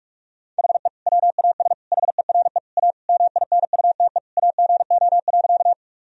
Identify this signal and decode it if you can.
Morse code, heard as a keyed tone, about 41 wpm, 700 Hz. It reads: HE WAS HERE A MINUTE AGO.
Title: Sentences from Top 1000 Words - Review Part 3 - 45wpm